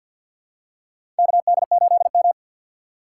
Morse code, heard as a keyed tone, about 53 wpm, 700 Hz. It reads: KD8M
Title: USA Call Signs - 4 Characters - 6x Character Spacing - 50wpm